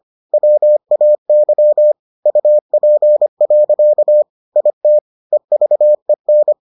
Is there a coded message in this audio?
Yes: Morse code from 0.3 to 6.5 s, heard as a keyed tone, about 25 wpm, 600 Hz. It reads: WAY UP. IT EVEN